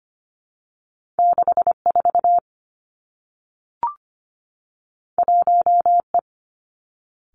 Morse code, heard as a keyed tone, about 25 wpm, 700 Hz. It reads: 64 1E
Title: Two Challenging Letter-Number Combinations 25wpm